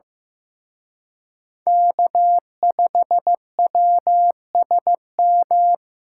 Morse code, heard as a keyed tone, about 18 wpm, 700 Hz. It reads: K5WSM
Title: Speed Racing - USA 1x3 Call Signs 15wpm